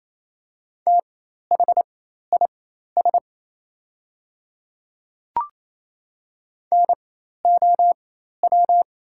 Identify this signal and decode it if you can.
Morse code, heard as a keyed tone, about 28 wpm, spaced as for 13 wpm, 690 Hz. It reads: THIS NOW
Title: 4x Character Spacing - Top 200 Words - 28wpm